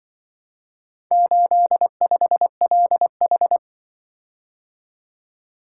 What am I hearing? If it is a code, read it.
Morse code, heard as a keyed tone, about 24 wpm, 700 Hz. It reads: 85LH
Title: Speed-Racing - Four Letter-Number Combinations - 20wpm